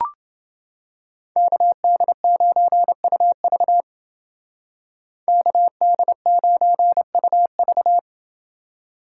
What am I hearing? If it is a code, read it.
Morse code, heard as a keyed tone, about 28 wpm, 700 Hz. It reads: KD9UV KD9UV